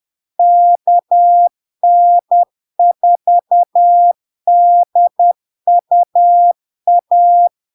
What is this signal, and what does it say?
Morse code, heard as a keyed tone, about 10 wpm, 700 Hz. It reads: KN4DUA
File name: LARK-FD-2022-Log-in-CW-10wpm-700-Hz